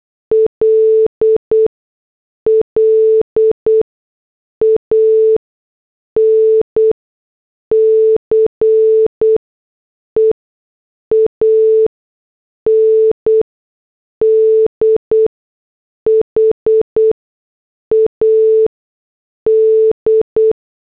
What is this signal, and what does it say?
Morse code, heard as a keyed tone, about 8 wpm, spaced as for 6 wpm, 440 Hz. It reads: LLANCE AND HAD